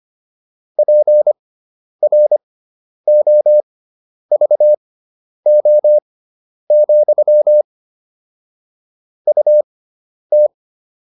Morse code, heard as a keyed tone, about 25 wpm, spaced as for 10 wpm, 600 Hz. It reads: PROVO, UT
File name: QSO301 10